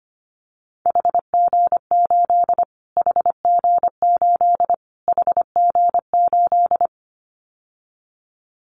Morse code, heard as a keyed tone, about 25 wpm, 700 Hz. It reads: HG8 HG8 HG8
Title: Callsign Prefixes for Top 25 DXCC - Repeated 3x - 25wpm